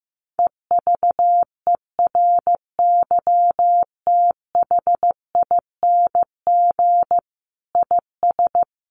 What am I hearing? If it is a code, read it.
Morse code, heard as a keyed tone, about 15 wpm, 700 Hz. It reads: EVERYTHING IS
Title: Sentences from Top 700 Words 15wpm